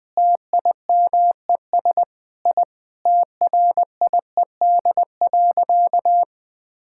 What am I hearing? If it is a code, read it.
Morse code, heard as a keyed tone, about 20 wpm, 700 Hz. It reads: TIMES I TRIED.
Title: Sentences with 8-Word Phrases - 20wpm